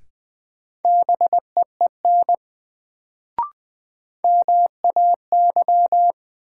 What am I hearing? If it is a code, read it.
Morse code, heard as a keyed tone, about 21 wpm, 700 Hz. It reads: BEEN MAY
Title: Sets of 2 Words - New Word Drill - 20wpm